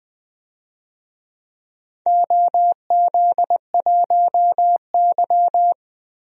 Morse code, heard as a keyed tone, about 20 wpm, 700 Hz. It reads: OZ1Y
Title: DXCC-100 Call Signs - 4 Characters - Repeated 2x - 20wpm